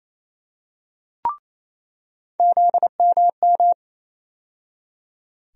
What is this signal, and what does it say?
Morse code, heard as a keyed tone, about 28 wpm, 700 Hz. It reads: ZMM
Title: USA Call Sign Postfixes - 3 Letters 28wpm